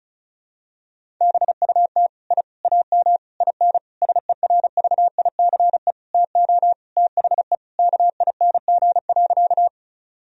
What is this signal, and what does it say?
Morse code, heard as a keyed tone, about 34 wpm, 700 Hz. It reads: BUT I AM IN SERVICE TO THE KING.
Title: Sentences from Top 1000 Words - Review Part 4 - 35wpm